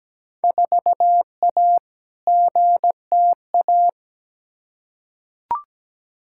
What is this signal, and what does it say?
Morse code, heard as a keyed tone, about 17 wpm, 700 Hz. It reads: 4A GTA
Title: Field Day Exchanges Encore - 17wpm